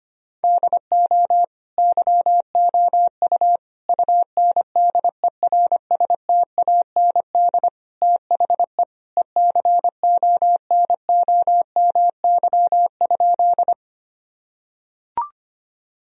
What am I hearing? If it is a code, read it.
Morse code, heard as a keyed tone, about 25 wpm, 700 Hz. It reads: DO YOU UNDERSTAND THE ECONOMY?
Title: Sentences from Top 1000 Words - Review Part 3 - 25wpm